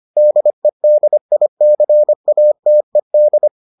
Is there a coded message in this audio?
Yes: Morse code from 0.2 to 3.5 s, heard as a keyed tone, about 25 wpm, 600 Hz. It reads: DEDICATED